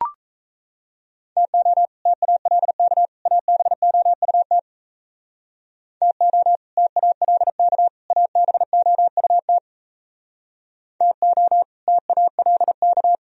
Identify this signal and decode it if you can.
Morse code, heard as a keyed tone, about 33 wpm, 700 Hz. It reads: TO TALK ABOUT TO TALK ABOUT TO TALK
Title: Speed-Racing - Sets of 3 Words 28wpm